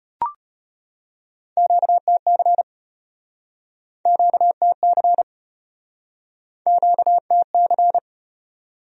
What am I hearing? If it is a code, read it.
Morse code, heard as a keyed tone, about 33 wpm, 700 Hz. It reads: QTC QTC QTC